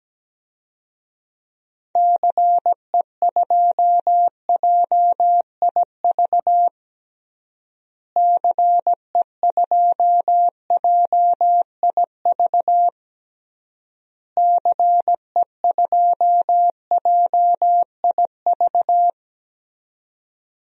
Morse code, heard as a keyed tone, about 17 wpm, 700 Hz. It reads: CE2JIV CE2JIV CE2JIV
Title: DXCC-100 Call Signs - 6 Characters - Repeated 3x - 17wpm